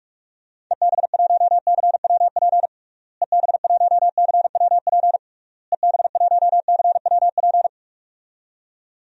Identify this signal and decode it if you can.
Morse code, heard as a keyed tone, about 46 wpm, 700 Hz. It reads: EB1CWP EB1CWP EB1CWP
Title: DXCC-100 Call Signs - 6 Characters - Repeated 3x - 45wpm